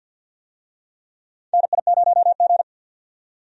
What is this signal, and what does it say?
Morse code, heard as a keyed tone, about 51 wpm, 700 Hz. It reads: NI0G
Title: USA Call Signs - 4 Characters - 50wpm